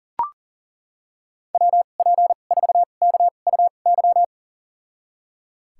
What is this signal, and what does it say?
Morse code, heard as a keyed tone, about 38 wpm, spaced as for 28 wpm, 700 Hz. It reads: WP4KUY
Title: USA Call Signs - 6 Characters - 2x Character Spacing - 40wpm